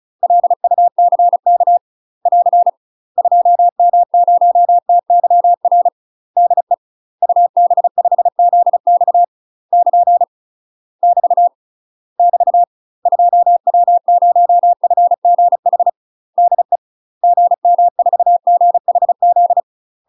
Morse code, heard as a keyed tone, about 35 wpm, 700 Hz. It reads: LUCK <AR> 2M0TYR DE UB5ZX <KN> <BT> <BT> 2W0FGH DE GM4GHZ
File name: UK MASTER QSOs TRAINING / 35WPM / UK MASTER QSOs 0002